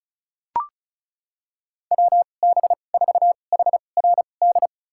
Morse code, heard as a keyed tone, about 34 wpm, spaced as for 25 wpm, 700 Hz. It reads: WB4HRD
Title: USA Call Signs - 2x Character Spacing - 35wpm